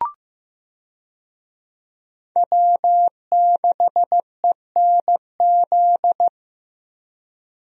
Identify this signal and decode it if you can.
Morse code, heard as a keyed tone, about 15 wpm, 700 Hz. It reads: W6ENZ
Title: USA Call Signs - 5 Characters - Repeated 3x - 15wpm